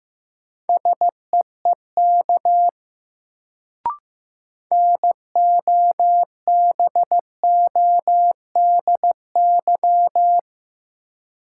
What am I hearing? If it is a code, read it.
Morse code, heard as a keyed tone, about 16 wpm, 700 Hz. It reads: SEEK NOBODY
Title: Sentences from Top 1600 Words - New Word Drill - 15wpm